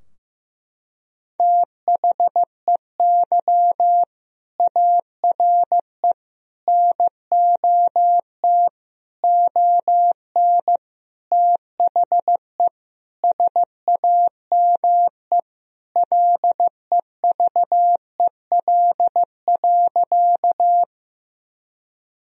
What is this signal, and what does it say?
Morse code, heard as a keyed tone, about 15 wpm, 700 Hz. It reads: THEY ARE NOT ON THE SAME LEVEL.
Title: Sentences from Top 1000 Words - Review Part 4 - 15wpm